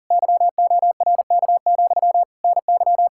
Morse code, heard as a keyed tone, about 50 wpm, 700 Hz. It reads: YORK, NY
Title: Speed-Racing Common QSO Phrases Encore - 40wpm